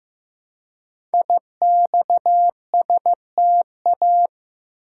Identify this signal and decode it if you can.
Morse code, heard as a keyed tone, about 15 wpm, 700 Hz. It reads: IXSTA
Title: Code Groups - 5 Letters - 15wpm